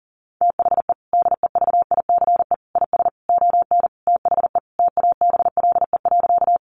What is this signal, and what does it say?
Morse code, heard as a keyed tone, about 40 wpm, 700 Hz. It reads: THE DEVICE IS ON THE TABLE.
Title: Sentences from Top 1500 Words 40wpm